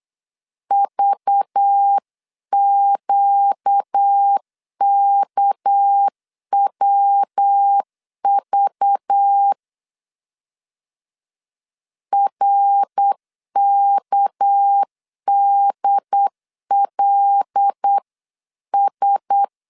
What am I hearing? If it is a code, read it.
Morse code, heard as a keyed tone, about 8 wpm, 800 Hz. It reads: VQKWV RKDLS
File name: 8wpm